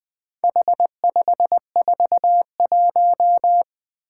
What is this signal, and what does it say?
Morse code, heard as a keyed tone, about 20 wpm, 700 Hz. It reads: H541
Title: Four Challenging Letter-Number Combinations 20wpm